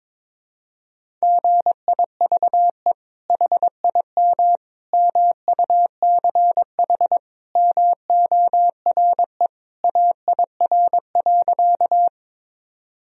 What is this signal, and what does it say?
Morse code, heard as a keyed tone, about 22 wpm, 700 Hz. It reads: GIVE HIM MUCH MORE AIR.
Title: Sentences from Top 1000 Words - Review Part 1 - 22wpm